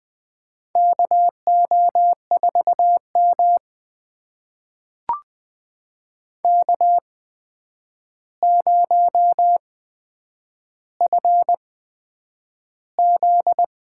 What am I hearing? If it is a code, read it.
Morse code, heard as a keyed tone, about 20 wpm, spaced as for 9 wpm, 700 Hz. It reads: KO4M K0FZ
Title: USA Call Signs - 4 Characters - 8x Character Spacing - 20wpm